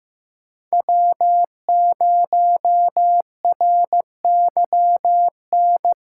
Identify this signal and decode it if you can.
Morse code, heard as a keyed tone, about 15 wpm, 700 Hz. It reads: W0RYN
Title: USA Call Signs - 5 Characters - 3x Character Spacing - 15wpm